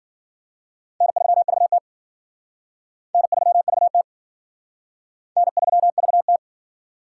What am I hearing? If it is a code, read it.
Morse code, heard as a keyed tone, about 50 wpm, 700 Hz. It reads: N3VT N3VT N3VT